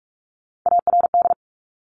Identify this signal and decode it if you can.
Morse code, heard as a keyed tone, about 46 wpm, 700 Hz. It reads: ARD
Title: Top Three Letter Combinations in Words 45wpm